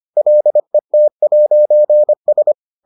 Morse code, heard as a keyed tone, about 25 wpm, 600 Hz. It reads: LET'S